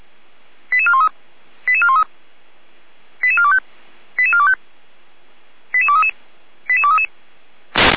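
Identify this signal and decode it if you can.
Five-tone selective call (ZVEI), heard as quick runs of five tones, tones of about 70 ms. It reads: 88413 88413 88425 88425 88233 88233
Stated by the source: ZVEI five-tone selective call received on an SDR.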